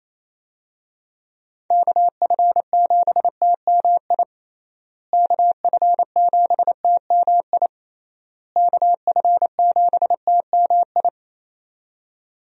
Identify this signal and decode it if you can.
Morse code, heard as a keyed tone, about 28 wpm, 700 Hz. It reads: KF7TMI KF7TMI KF7TMI
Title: USA Call Signs - 6 Characters - Repeated 3x - 28wpm